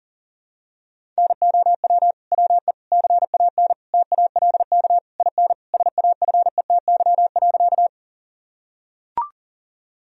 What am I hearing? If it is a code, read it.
Morse code, heard as a keyed tone, about 40 wpm, 700 Hz. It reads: NOW WE CAN TALK IN SAFETY.
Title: Sentences from Top 1000 Words - Review Part 4 - 40wpm